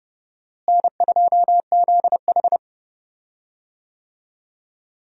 Morse code, heard as a keyed tone, about 30 wpm, 700 Hz. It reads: N2ZH
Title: USA Call Signs - 4 Characters - Repeated 3x - 30wpm